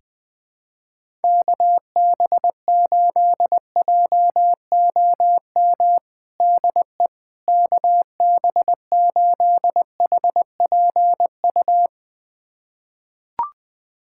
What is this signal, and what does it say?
Morse code, heard as a keyed tone, about 21 wpm, 700 Hz. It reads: KB8JOM DE KB8HPU
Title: USA Callsign DE Callsign - 6 Characters - 20wpm